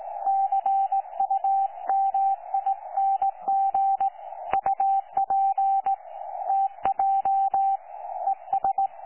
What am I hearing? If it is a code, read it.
Morse code, heard as a keyed tone, about 17 wpm, 780 Hz. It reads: CUMING UP TJ ES
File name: 12 39Z 18088.2kHz utah aa7fv k9tjl